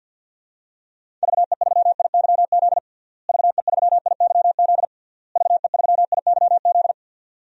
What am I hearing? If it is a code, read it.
Morse code, heard as a keyed tone, about 50 wpm, 700 Hz. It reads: VE3IYZ VE3IYZ VE3IYZ
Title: DXCC-100 Call Signs - 6 Characters - Repeated 3x - 50wpm